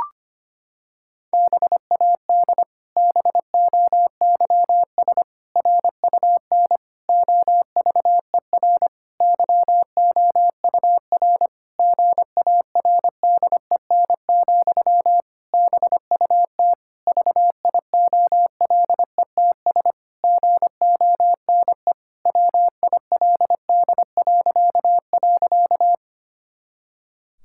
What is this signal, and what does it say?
Morse code, heard as a keyed tone, about 25 wpm, 700 Hz. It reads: BAD BOYS RUN OVER YOUR GARDEN, BUT VIOLETS GONE WILD..
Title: Pangram Challenge v2 - 25wpm